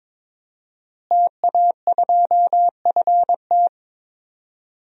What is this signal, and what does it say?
Morse code, heard as a keyed tone, about 22 wpm, 700 Hz. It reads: TA2FT
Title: DXCC-100 Call Signs - 5 Characters - Repeated 2x - 22wpm